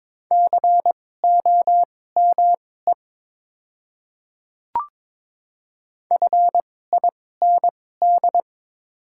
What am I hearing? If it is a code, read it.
Morse code, heard as a keyed tone, about 22 wpm, spaced as for 16 wpm, 700 Hz. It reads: COME FIND
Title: 2x Character Spacing - Top 300 Words - 22wpm